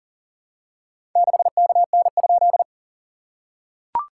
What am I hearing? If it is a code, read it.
Morse code, heard as a keyed tone, about 43 wpm, 700 Hz. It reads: 6KN?
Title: SOTA QSO Practice 40wpm